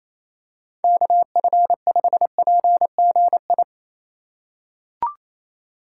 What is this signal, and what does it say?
Morse code, heard as a keyed tone, about 28 wpm, 700 Hz. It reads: KF5PGI
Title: USA Call Signs - 6 Characters - 2x Character Spacing - 28wpm